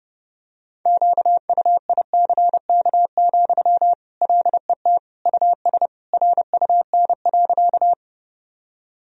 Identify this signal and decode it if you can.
Morse code, heard as a keyed tone, about 32 wpm, 700 Hz. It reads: QUICK, LET US RUN.
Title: Sentences from Top 1000 Words - Review Part 1 - 30wpm